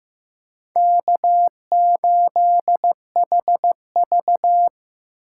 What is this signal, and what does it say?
Morse code, heard as a keyed tone, about 15 wpm, 700 Hz. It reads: K8HV
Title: USA Call Signs - 4 Characters - 8x Character Spacing - 15wpm